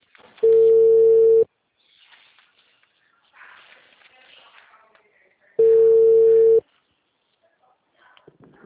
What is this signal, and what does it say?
Ringback tone (China, 450 Hz).